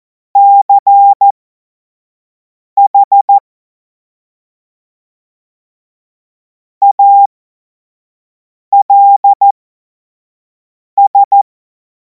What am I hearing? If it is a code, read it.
Morse code, heard as a keyed tone, about 14 wpm, spaced as for 5 wpm, 800 Hz. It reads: CH ALS
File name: Morsekurs / Witze / Witz 24. Februar 2026 25 F70